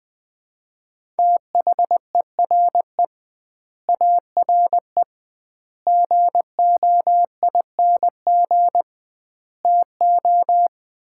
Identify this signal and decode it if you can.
Morse code, heard as a keyed tone, about 20 wpm, 700 Hz. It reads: THERE ARE GOING TO
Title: 2x Word-Spacing - Sets of 5 Words - 20wpm